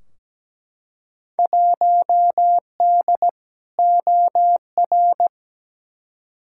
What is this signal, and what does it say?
Morse code, heard as a keyed tone, about 17 wpm, 700 Hz. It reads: 1D OR
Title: Field Day Exchanges - 17wpm